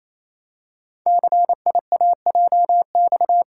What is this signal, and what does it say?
Morse code, heard as a keyed tone, about 29 wpm, 700 Hz. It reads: CIAJX
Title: Code Groups - 5 Letters - 28wpm